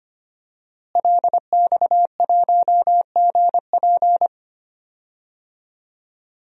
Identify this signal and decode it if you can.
Morse code, heard as a keyed tone, about 25 wpm, 700 Hz. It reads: LX1GP
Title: DXCC-100 Call Signs - 5 Characters - Repeated 2x - 25wpm